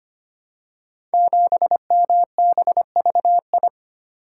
Morse code, heard as a keyed tone, about 25 wpm, 700 Hz. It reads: 7MBVI
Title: Code Groups - 5 Letter:Numbers - 25wpm